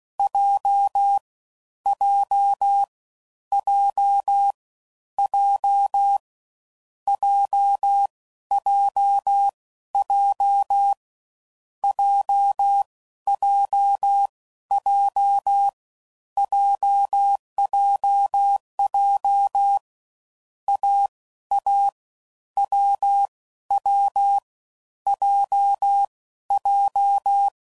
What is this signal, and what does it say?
Morse code, heard as a keyed tone, about 16 wpm, spaced as for 12 wpm, 800 Hz. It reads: J J J J JJJ JJJ JJJ AA WW JJ
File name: Combinations / Difficult Combos / AWJ1